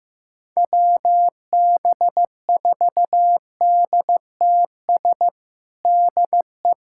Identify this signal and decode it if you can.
Morse code, heard as a keyed tone, about 15 wpm, 700 Hz. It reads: WB4DTS DE
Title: USA Callsign DE Callsign - 6 Characters - 15wpm